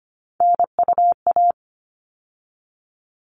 Morse code, heard as a keyed tone, about 25 wpm, 700 Hz. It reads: NUA
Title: All Cut Numbers - 3-Digit 25wpm